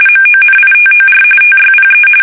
RTTY, about 45.45 baud. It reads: MONITOR TYPE
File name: fsk850 / usnavy3